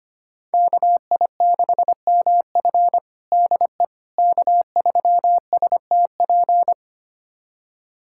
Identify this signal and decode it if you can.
Morse code, heard as a keyed tone, about 25 wpm, 700 Hz. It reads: KI6MF DE K3STP
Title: USA Callsign DE Callsign - 5 Characters - 25wpm